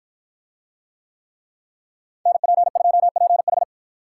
Morse code, heard as a keyed tone, about 53 wpm, 700 Hz. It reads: NP2PH